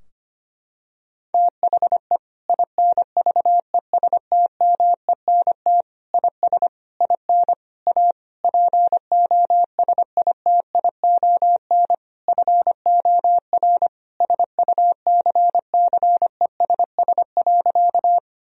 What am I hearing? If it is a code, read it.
Morse code, heard as a keyed tone, about 25 wpm, 700 Hz. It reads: THE INVESTMENT IS IN A POSITION FOR SUCCESS.
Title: Sentences from Top 1000 Words - Review Part 4 - 25wpm